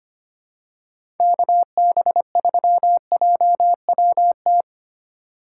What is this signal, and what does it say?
Morse code, heard as a keyed tone, about 25 wpm, 700 Hz. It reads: KB3JWT
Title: USA Call Signs - 3x Character Spacing - 25wpm